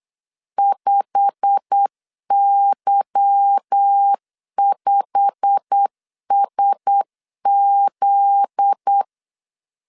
Morse code, heard as a keyed tone, about 8 wpm, 800 Hz. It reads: 5Y5SZ
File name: LEZIONI / 8 YJ45 / 8 YJ45 8wpm 3214